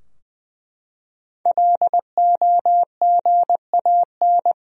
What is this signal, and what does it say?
Morse code, heard as a keyed tone, about 20 wpm, 700 Hz. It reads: LOGAN
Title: USA Common Names - 20wpm